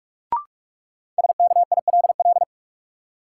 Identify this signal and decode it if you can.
Morse code, heard as a keyed tone, about 46 wpm, 700 Hz. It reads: SKILL
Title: Sentences from Top 1200 Words - New Word Drill - 45wpm